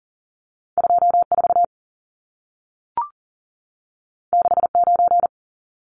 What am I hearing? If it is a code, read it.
Morse code, heard as a keyed tone, about 40 wpm, 700 Hz. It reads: 24 69
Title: Numbers - 2-Digit 40wpm